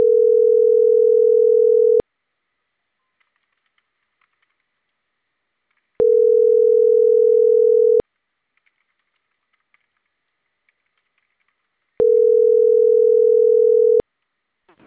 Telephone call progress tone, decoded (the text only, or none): ringback tone